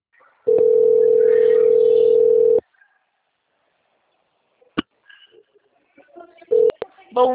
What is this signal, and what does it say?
Ringback tone (North America), 440 + 480 Hz.
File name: out- - - - - .44593